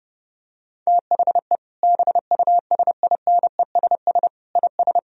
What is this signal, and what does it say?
Morse code, heard as a keyed tone, about 30 wpm, 700 Hz. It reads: THE BUSINESS IS
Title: Sentences from Top 1000 Words - Review Part 4 - 30wpm